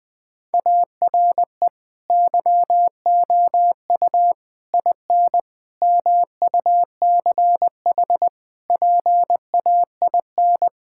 Morse code, heard as a keyed tone, about 20 wpm, 700 Hz. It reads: ARE YOU IN MUCH PAIN
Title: Sentences from Top 1600 Words 20wpm